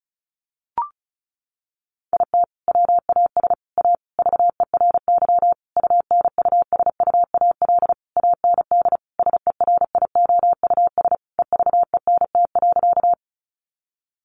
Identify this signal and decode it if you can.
Morse code, heard as a keyed tone, about 34 wpm, 700 Hz. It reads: IT WAS A VERY UNUSUAL AND SERIOUS EVENT.
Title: Sentences from Top 1400 Words 35wpm